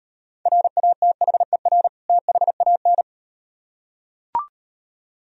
Morse code, heard as a keyed tone, about 38 wpm, 700 Hz. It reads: RATHER THAN